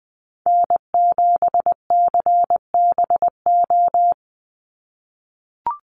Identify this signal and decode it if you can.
Morse code, heard as a keyed tone, about 20 wpm, 700 Hz. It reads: N7CBO
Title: USA Call Signs 20wpm